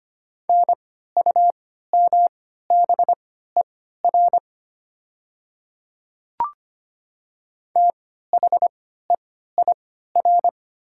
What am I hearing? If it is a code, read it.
Morse code, heard as a keyed tone, about 25 wpm, spaced as for 14 wpm, 700 Hz. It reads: NUMBER THEIR
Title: 3x Character Spacing - Top 100 Words - 25wpm